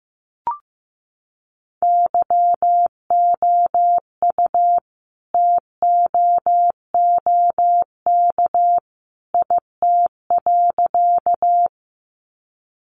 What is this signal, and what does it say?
Morse code, heard as a keyed tone, about 15 wpm, 700 Hz. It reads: YOU TOOK IT.
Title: Sentences from Top 300 Words 15wpm